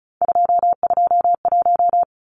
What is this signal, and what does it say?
Morse code, heard as a keyed tone, about 34 wpm, 700 Hz. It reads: 221
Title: Numbers - 3-Digit 35wpm